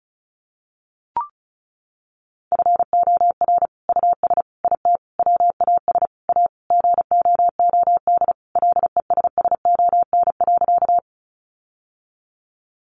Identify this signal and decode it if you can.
Morse code, heard as a keyed tone, about 34 wpm, 700 Hz. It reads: FOR US IT WAS A GOOD LESSON.
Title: Sentences from Top 1200 Words 35wpm